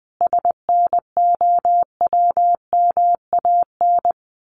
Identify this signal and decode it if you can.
Morse code, heard as a keyed tone, about 19 wpm, 700 Hz. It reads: SNOWMAN
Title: Easy Phrases 20wpm